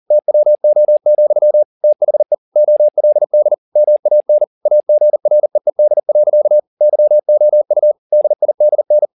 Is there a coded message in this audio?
Yes: Morse code from 0.1 to 9.1 s, heard as a keyed tone, about 40 wpm, 600 Hz. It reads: TWO, THE OLD MAN AGREED. YOU DIDN